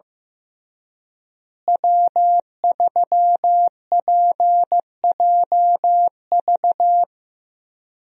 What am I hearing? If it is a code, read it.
Morse code, heard as a keyed tone, about 15 wpm, 700 Hz. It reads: W3PJV
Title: USA Call Signs - 5 Characters - Repeated 3x - 15wpm